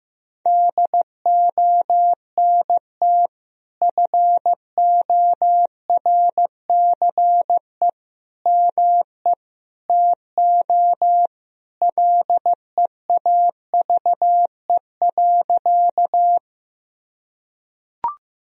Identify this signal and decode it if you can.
Morse code, heard as a keyed tone, about 15 wpm, 700 Hz. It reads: DONT FORCE ME TO LEAVE.
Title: Sentences from Top 1000 Words - Review Part 3 - 15wpm